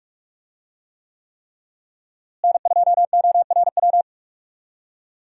Morse code, heard as a keyed tone, about 42 wpm, 700 Hz. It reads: N2ORW
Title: USA Call Signs - 5 Characters - Repeated 3x - 45wpm